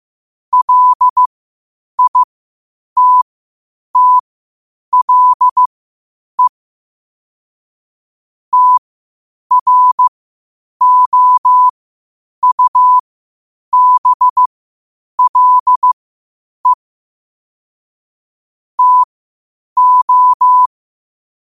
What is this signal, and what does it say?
Morse code, heard as a keyed tone, about 15 wpm, spaced as for 8 wpm, 1000 Hz. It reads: LITTLE TROUBLE TO
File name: qotdfiles / 8wpm